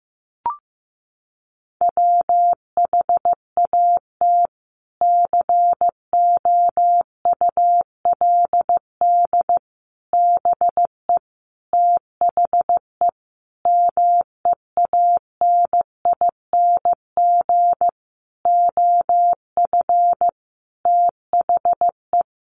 Morse code, heard as a keyed tone, about 15 wpm, 700 Hz. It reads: WHAT COULD BE THE MEANING OF THE
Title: Sentences from Top 900 Words 15wpm